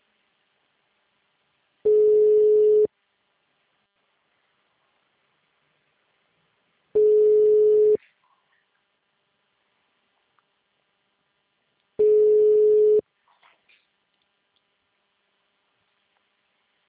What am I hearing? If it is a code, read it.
Ringback tone (Europe, 425 Hz).
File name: out- -2016- - - .8383